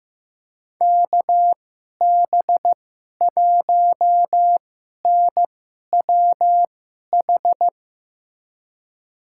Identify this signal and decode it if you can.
Morse code, heard as a keyed tone, about 15 wpm, spaced as for 11 wpm, 700 Hz. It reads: KB1NWH
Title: USA Call Signs - 2x Character Spacing - 15wpm